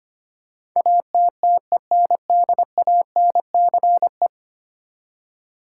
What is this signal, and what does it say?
Morse code, heard as a keyed tone, about 25 wpm, 700 Hz. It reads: ATTENDANCE
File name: Ten Letter Words - 25wpm